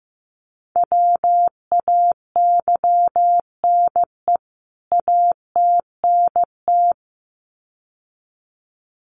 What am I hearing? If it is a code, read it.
Morse code, heard as a keyed tone, about 15 wpm, 700 Hz. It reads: WAYNE ATNT
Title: CWT Exchange 15wpm